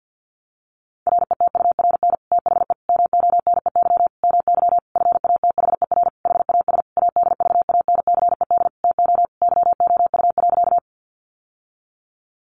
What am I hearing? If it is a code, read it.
Morse code, heard as a keyed tone, about 50 wpm, 700 Hz. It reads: RETURN THE MONEY MY FATHER HAS ADVANCED TO YOU.
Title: Sentences from Top 1200 Words 50wpm